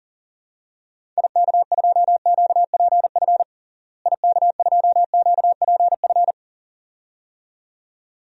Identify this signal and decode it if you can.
Morse code, heard as a keyed tone, about 40 wpm, 700 Hz. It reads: IK2QPF IK2QPF
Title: DXCC-100 Call Signs - 6 Characters - Repeated 2x - 40wpm